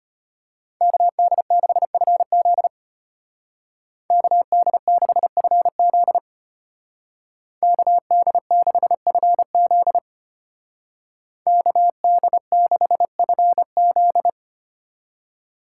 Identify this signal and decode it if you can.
Morse code, heard as a keyed tone, about 30 wpm, 700 Hz. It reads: KD6FZ KD6FZ KD6FZ KD6FZ
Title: Speed Racing - USA 2x2 Call Signs 25wpm